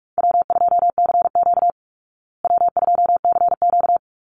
Morse code, heard as a keyed tone, about 44 wpm, 700 Hz. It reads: W2CQ W2CQ
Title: Easy Call Signs Repeated 2x - 45wpm